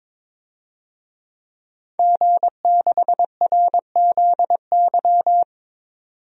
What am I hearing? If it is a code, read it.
Morse code, heard as a keyed tone, about 22 wpm, 700 Hz. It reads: G6RZY
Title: DXCC-100 Call Signs - 5 Characters - 22wpm